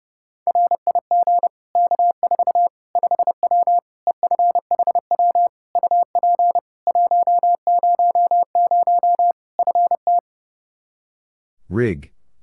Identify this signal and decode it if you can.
Morse code, heard as a keyed tone, about 30 wpm, 700 Hz. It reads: RIG K4 5W EFHW UP 100 FT
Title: Practice QSO - Exchange 2 - Info - 30wpm